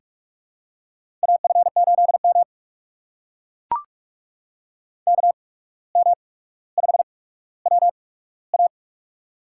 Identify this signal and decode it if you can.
Morse code, heard as a keyed tone, about 44 wpm, spaced as for 18 wpm, 700 Hz. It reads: AF8M KM5WA
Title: USA Call Signs - 8x Character Spacing - 45wpm